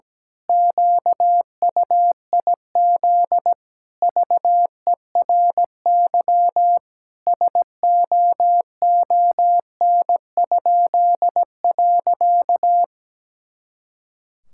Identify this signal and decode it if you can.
Morse code, heard as a keyed tone, about 17 wpm, 700 Hz. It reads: QUIZ VERY SOON?.
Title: Pangram Challenge v2 - 17wpm